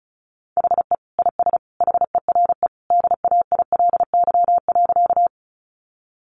Morse code, heard as a keyed tone, about 34 wpm, 700 Hz. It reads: HE IS HERE DAILY.
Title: Sentences from Top 1100 Words 35wpm